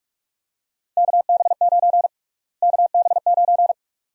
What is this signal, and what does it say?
Morse code, heard as a keyed tone, about 47 wpm, 700 Hz. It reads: KB9 KB9
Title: USA Call Sign Prefixes - Repeated 2x - 45wpm